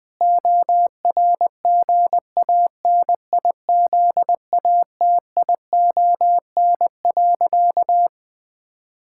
Morse code, heard as a keyed tone, about 20 wpm, 700 Hz. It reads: ORGANIZATION.
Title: Sentences from Top 1000 Words - Review Part 6 - 20wpm